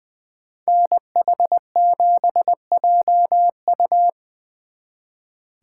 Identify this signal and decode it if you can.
Morse code, heard as a keyed tone, about 23 wpm, 700 Hz. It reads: NH7JU
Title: Speed Racing - USA Call Signs 20wpm